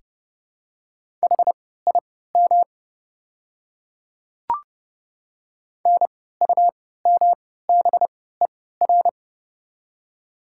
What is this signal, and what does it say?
Morse code, heard as a keyed tone, about 30 wpm, spaced as for 17 wpm, 700 Hz. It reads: HIM NUMBER
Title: 3x Character Spacing - Top 100 Words - 30wpm